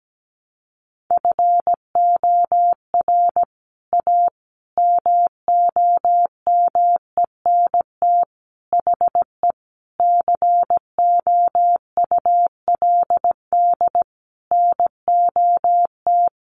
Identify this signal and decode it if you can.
Morse code, heard as a keyed tone, about 17 wpm, 700 Hz. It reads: FOR A MOMENT HE COULD NOT
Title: Sentences from Top 1100 Words 17wpm